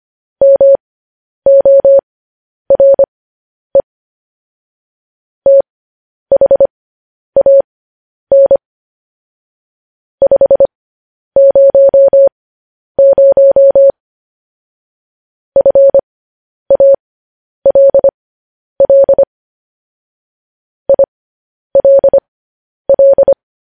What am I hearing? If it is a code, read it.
Morse code, heard as a keyed tone, about 25 wpm, spaced as for 10 wpm, 550 Hz. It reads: MORE THAN 500 FALL ILL
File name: podcast / News headlines-1x-25x10